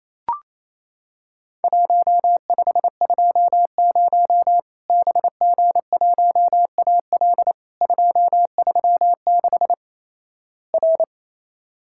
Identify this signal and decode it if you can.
Morse code, heard as a keyed tone, about 28 wpm, 700 Hz. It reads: 1520 BG1AL 236 R
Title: Worked All Europe DX Contest - QTC Exchange - 28wpm